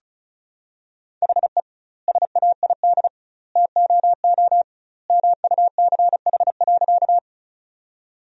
Morse code, heard as a keyed tone, about 36 wpm, 700 Hz. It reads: HE SAID TOO MUCH.
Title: 2x Word-Spacing - Sentences from Top 200 Words - 35wpm